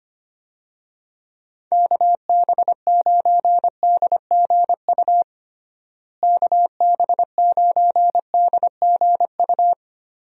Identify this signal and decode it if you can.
Morse code, heard as a keyed tone, about 25 wpm, 700 Hz. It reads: KB9DGU KB9DGU
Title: USA Call Signs - 6 Characters - Repeated 3x - 25wpm